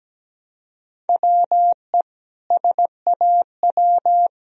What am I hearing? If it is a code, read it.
Morse code, heard as a keyed tone, about 17 wpm, 700 Hz. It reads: WE SAW
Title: Sentences from Top 1000 Words - Review Part 4 - 17wpm